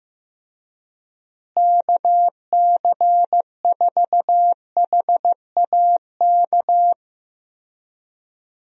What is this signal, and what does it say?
Morse code, heard as a keyed tone, about 15 wpm, 700 Hz. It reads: KC4HAK
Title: USA Call Signs - 6 Characters - Repeated 2x - 15wpm